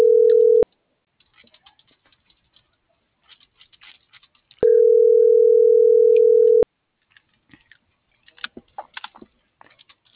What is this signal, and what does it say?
Ringback tone (North America), 440 + 480 Hz.